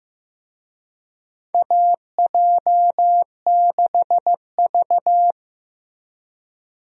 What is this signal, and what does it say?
Morse code, heard as a keyed tone, about 15 wpm, 700 Hz. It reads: AJ6V
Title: USA Call Signs - 4 Characters - Repeated 3x - 15wpm